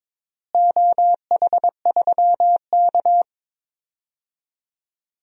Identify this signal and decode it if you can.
Morse code, heard as a keyed tone, about 22 wpm, 700 Hz. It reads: OH3K
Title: DXCC-100 Call Signs - 4 Characters - Repeated 3x - 22wpm